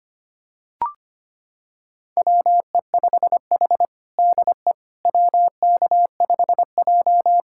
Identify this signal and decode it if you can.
Morse code, heard as a keyed tone, about 25 wpm, 700 Hz. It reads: WE5H DE WK5J
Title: USA Callsign DE Callsign - 4 Characters - 25wpm